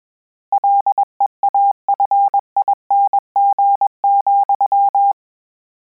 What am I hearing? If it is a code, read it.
Morse code, heard as a keyed tone, about 21 wpm, 800 Hz. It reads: LEAFING,
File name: Combinations / Combinations-mp3s / Combinations@20